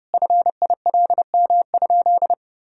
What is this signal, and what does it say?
Morse code, heard as a keyed tone, about 32 wpm, 700 Hz. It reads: FILM?
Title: Sentences from Top 1000 Words - Review Part 3 - 30wpm